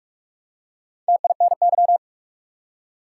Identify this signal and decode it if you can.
Morse code, heard as a keyed tone, about 47 wpm, 700 Hz. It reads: TINY
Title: Four Letter Words - 45wpm